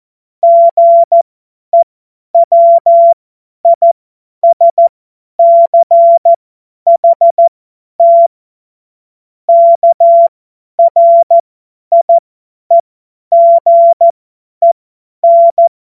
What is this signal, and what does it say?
Morse code, heard as a keyed tone, about 14 wpm, spaced as for 10 wpm, 680 Hz. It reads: GEWISCHT KRIEGEN
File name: Morsekurs / Witze / tz 08. November 2022 50 F70